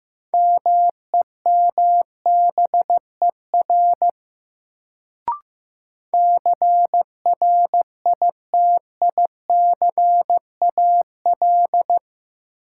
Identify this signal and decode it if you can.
Morse code, heard as a keyed tone, about 15 wpm, 700 Hz. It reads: MEMBER CRITICAL
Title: Sentences from Top 1100 Words - New Word Drill - 15wpm